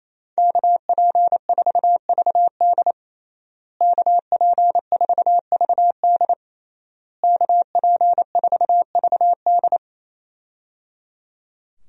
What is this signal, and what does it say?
Morse code, heard as a keyed tone, about 28 wpm, 700 Hz. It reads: KP4VD KP4VD KP4VD
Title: DXCC-100 Call Signs - 5 Characters - Repeated 3x - 28wpm